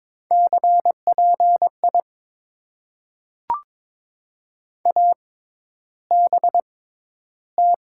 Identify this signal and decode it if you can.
Morse code, heard as a keyed tone, about 22 wpm, spaced as for 11 wpm, 700 Hz. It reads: CPI ABT
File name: Top 90 QSO Elements - 6x Character Spacing - 22wpm